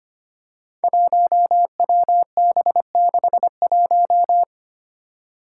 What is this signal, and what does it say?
Morse code, heard as a keyed tone, about 25 wpm, 700 Hz. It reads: 1WB61
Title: Five Challenging Letter-Number Combinations 25wpm